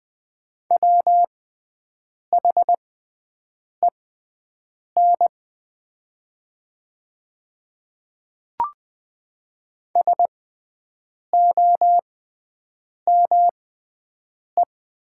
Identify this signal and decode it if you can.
Morse code, heard as a keyed tone, about 20 wpm, spaced as for 7 wpm, 700 Hz. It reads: WHEN SOME
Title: 6x Character Spacing - Top 200 Words - 20wpm